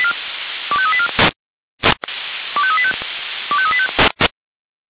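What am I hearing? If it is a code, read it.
Five-tone selective call (ZVEI), heard as quick runs of five tones, tones of about 70 ms. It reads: 25384 25385 25385
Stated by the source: Five-tone ZVEI selective call received by SDR.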